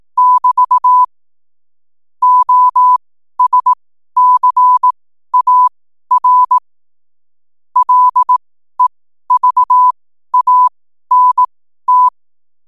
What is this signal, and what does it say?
Morse code, heard as a keyed tone, about 18 wpm, spaced as for 12 wpm, 1000 Hz. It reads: <BT> OSCAR LEVANT
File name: n7net / 12wpm